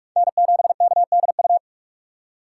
Morse code, heard as a keyed tone, about 45 wpm, 690 Hz. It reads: N7KDU
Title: USA Call Signs - 5 Characters - 6x Character Spacing - 45wpm